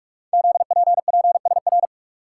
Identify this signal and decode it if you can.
Morse code, heard as a keyed tone, about 44 wpm, 700 Hz. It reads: ZPPSR
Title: Code Groups - 5 Letters - 45wpm